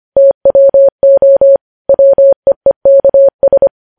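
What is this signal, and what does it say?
Morse code, heard as a keyed tone, about 25 wpm, 550 Hz. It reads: TWO WEEKS
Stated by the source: Morse practice of news headlines at 25wpm.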